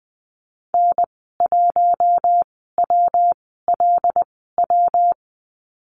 Morse code, heard as a keyed tone, about 20 wpm, spaced as for 15 wpm, 700 Hz. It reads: N1WLW
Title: 147 Lesson 24g - Letter y - Call Signs 20wpm